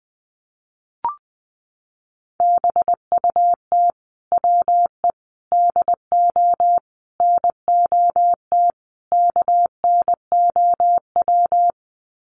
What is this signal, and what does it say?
Morse code, heard as a keyed tone, about 20 wpm, 700 Hz. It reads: BUT WE DO NOT KNOW
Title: Sets of 5 Words Encore2 20wpm